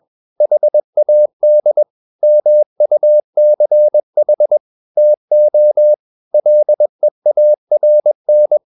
Morse code, heard as a keyed tone, about 21 wpm, 600 Hz. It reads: HAD MUCH TO LEARN
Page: CW med Gnister